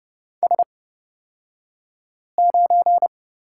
Morse code, heard as a keyed tone, about 30 wpm, spaced as for 5 wpm, 700 Hz. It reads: S9
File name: Lesson 38d - Letter q - Two Characters 30wpm